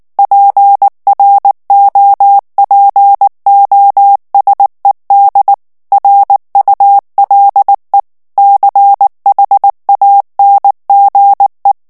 Morse code, heard as a keyed tone, about 19 wpm, 800 Hz. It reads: PROPOSED RULE CHANGE
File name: ANGLAIS texte clair 3 vitesse 18 mots minute